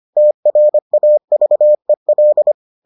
Morse code, heard as a keyed tone, about 25 wpm, 600 Hz. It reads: TRAVEL